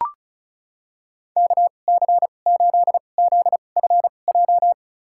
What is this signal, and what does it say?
Morse code, heard as a keyed tone, about 34 wpm, spaced as for 25 wpm, 700 Hz. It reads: KC8ZFJ
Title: USA Call Signs - 2x Character Spacing - 35wpm